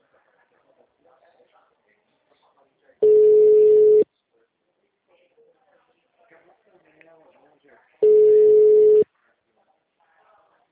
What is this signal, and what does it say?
Ringback tone (Europe, 425 Hz), with speech.